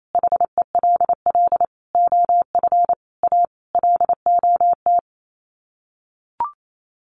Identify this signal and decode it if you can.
Morse code, heard as a keyed tone, about 28 wpm, 700 Hz. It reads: HELL OF A LOT
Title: Sets of 5 Words Encore 28wpm